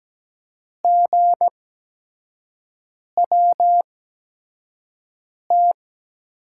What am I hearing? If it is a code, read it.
Morse code, heard as a keyed tone, about 17 wpm, spaced as for 5 wpm, 700 Hz. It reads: GWT
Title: USA Call Signs - 5 Characters - 8x Character Spacing - 17wpm